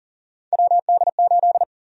Morse code, heard as a keyed tone, about 38 wpm, 700 Hz. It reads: WD8
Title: USA Call Sign Prefixes - Repeated 2x - 40wpm